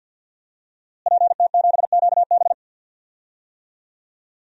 Morse code, heard as a keyed tone, about 47 wpm, 700 Hz. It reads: PT7QB
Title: DXCC-100 Call Signs - 5 Characters - Repeated 3x - 50wpm